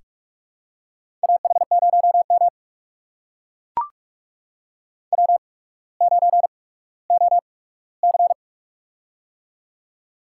Morse code, heard as a keyed tone, about 44 wpm, spaced as for 19 wpm, 700 Hz. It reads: AH0M W9OC
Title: USA Call Signs - 4 Characters - 8x Character Spacing - 45wpm